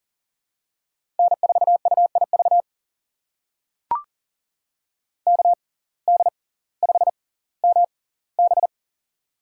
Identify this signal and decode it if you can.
Morse code, heard as a keyed tone, about 39 wpm, spaced as for 20 wpm, 700 Hz. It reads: N4UIV KD5MB
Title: USA Call Signs - 5 Characters - 6x Character Spacing - 40wpm